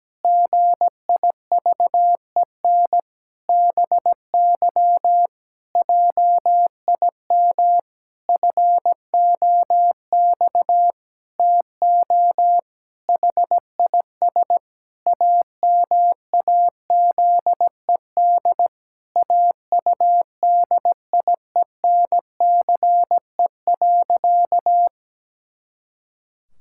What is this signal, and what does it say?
Morse code, heard as a keyed tone, about 17 wpm, 700 Hz. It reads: GIVEN BY JIM FOX TO HIS AMAZED AUDIENCE.
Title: Pangram Challenge v2 - 17wpm